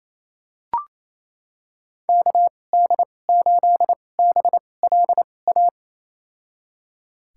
Morse code, heard as a keyed tone, about 28 wpm, spaced as for 20 wpm, 700 Hz. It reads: KD8BLA
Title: USA Call Signs - 2x Character Spacing - 28wpm